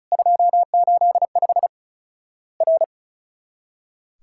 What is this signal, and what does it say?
Morse code, heard as a keyed tone, about 36 wpm, 700 Hz. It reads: 285 R
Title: Worked All Europe DX Contest - QTC Exchange - 35wpm